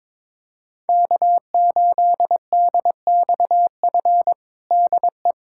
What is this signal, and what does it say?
Morse code, heard as a keyed tone, about 22 wpm, 700 Hz. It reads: K8DXF DE
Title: USA Callsign DE Callsign - 5 Characters - 22wpm